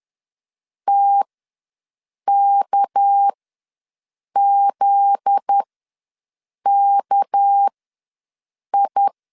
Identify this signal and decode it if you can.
Morse code, heard as a keyed tone, about 10 wpm, spaced as for 6 wpm, 800 Hz. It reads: TKZKI
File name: LEZIONI / 7 GF9Z / 7 GF9Z ws 10wpm 3919